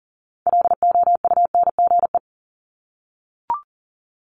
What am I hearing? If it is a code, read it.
Morse code, heard as a keyed tone, about 40 wpm, 700 Hz. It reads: LOUNGE
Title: Multi-letter Phonemes Repeated 3x 40wpm